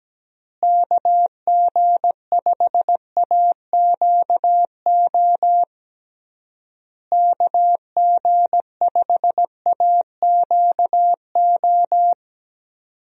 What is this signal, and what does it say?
Morse code, heard as a keyed tone, about 17 wpm, 700 Hz. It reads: KG5AQO KG5AQO
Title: USA Call Signs - 6 Characters - Repeated 3x - 17wpm